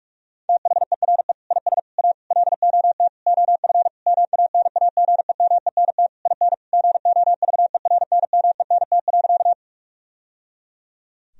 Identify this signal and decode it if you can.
Morse code, heard as a keyed tone, about 41 wpm, 700 Hz. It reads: THERE IS A LOT OF MANAGEMENT IN GOVERNMENT.
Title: Sentences from Top 1000 Words - Review Part 4 - 45wpm